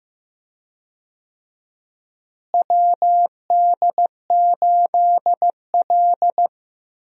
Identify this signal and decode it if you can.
Morse code, heard as a keyed tone, about 15 wpm, 700 Hz. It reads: WD8L
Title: USA Call Signs - 4 Characters - Repeated 2x - 15wpm